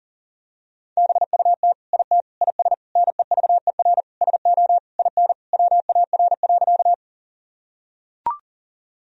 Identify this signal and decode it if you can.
Morse code, heard as a keyed tone, about 40 wpm, 700 Hz. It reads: BUT IT IS NEVER SO IN WAR.
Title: Sentences from Top 1000 Words - Review Part 2 - 40wpm